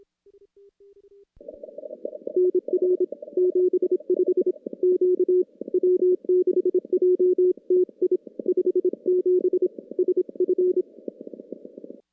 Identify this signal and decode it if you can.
Morse code, heard as a keyed tone, about 26 wpm, 370 Hz. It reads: NR 75 Q W6JTI 57 SF
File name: contests / CW-SS-k5tr- -2300